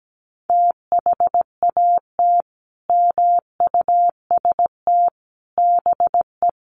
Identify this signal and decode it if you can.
Morse code, heard as a keyed tone, about 17 wpm, 700 Hz. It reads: THAT MUST BE
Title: Sentences from Top 400 Words 17wpm